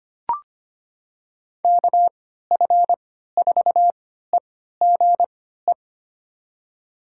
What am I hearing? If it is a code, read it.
Morse code, heard as a keyed tone, about 25 wpm, spaced as for 14 wpm, 700 Hz. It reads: KF4EGE
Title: USA Call Signs - 6 Characters - 3x Character Spacing - 25wpm